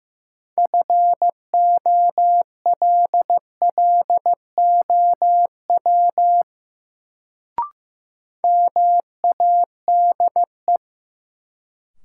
Morse code, heard as a keyed tone, about 15 wpm, 700 Hz. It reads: FOLLOW MADE
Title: Sentences from Top 200 Words - New Word Drill - 15wpm